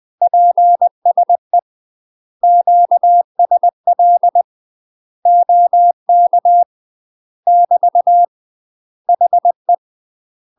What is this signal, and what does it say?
Morse code, heard as a keyed tone, about 20 wpm, 700 Hz. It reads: PSE QSL OK <BT> HE
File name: UK MASTER QSOs TRAINING / 20WPM / UK MASTER QSOs 0062